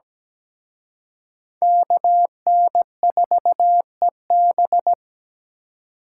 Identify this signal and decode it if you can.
Morse code, heard as a keyed tone, about 20 wpm, 700 Hz. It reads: KN4EB
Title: Speed Racing - USA 2x2 Call Signs 17wpm